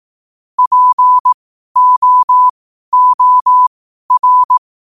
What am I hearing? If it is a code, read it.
Morse code, heard as a keyed tone, about 18 wpm, spaced as for 12 wpm, 1000 Hz. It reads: POOR
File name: qotdfiles / 12wpm